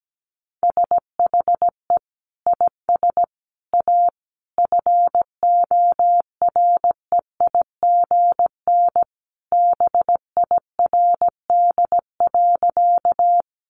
Morse code, heard as a keyed tone, about 17 wpm, 700 Hz. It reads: SHE IS A FOREIGN BIRD.
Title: Sentences from Top 1400 Words 17wpm